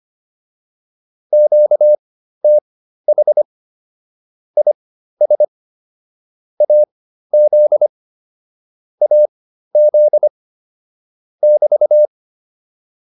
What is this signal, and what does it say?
Morse code, heard as a keyed tone, about 25 wpm, spaced as for 13 wpm, 600 Hz. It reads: QTH IS AZ AZ <BT>